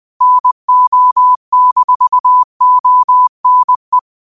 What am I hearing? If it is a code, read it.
Morse code, heard as a keyed tone, about 20 wpm, 1000 Hz. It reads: NO-ONE
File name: qotdfiles / 20wpm